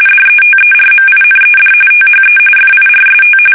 RTTY, about 45.45 baud. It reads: DETERMINED THE FIRST
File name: fsk850 / battleshipnj